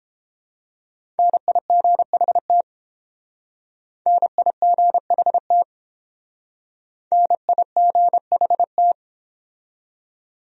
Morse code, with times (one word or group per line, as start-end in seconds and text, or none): 1.2-2.6 NIGHT
4.1-5.6 NIGHT
7.1-8.9 NIGHT